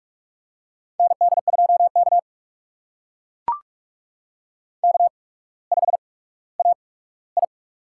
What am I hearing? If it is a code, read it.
Morse code, heard as a keyed tone, about 44 wpm, spaced as for 19 wpm, 700 Hz. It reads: ND2K K5AI
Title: USA Call Signs - 4 Characters - 8x Character Spacing - 45wpm